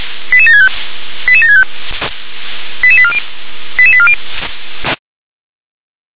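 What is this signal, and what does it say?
Five-tone selective call (ZVEI), heard as quick runs of five tones, tones of about 70 ms. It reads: 88765 88765 88840 88840
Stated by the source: Five-tone ZVEI selective call received on an SDR.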